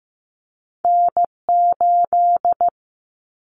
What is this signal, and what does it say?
Morse code, heard as a keyed tone, about 15 wpm, 700 Hz. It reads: N8
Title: Callsign Prefixes for Top 25 DXCC - Repeated 3x - 15wpm